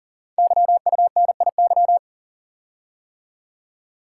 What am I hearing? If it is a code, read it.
Morse code, heard as a keyed tone, about 43 wpm, 700 Hz. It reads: YUNIY
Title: Code Groups - 5 Letters - 40wpm